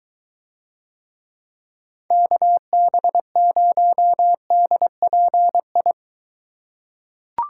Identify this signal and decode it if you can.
Morse code, heard as a keyed tone, about 23 wpm, 700 Hz. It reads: KB0DPI